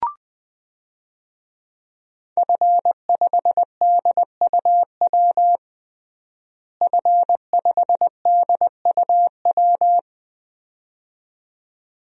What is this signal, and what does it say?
Morse code, heard as a keyed tone, about 20 wpm, 700 Hz. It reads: F5DUW F5DUW
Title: DXCC-100 Call Signs - 5 Characters - Repeated 2x - 20wpm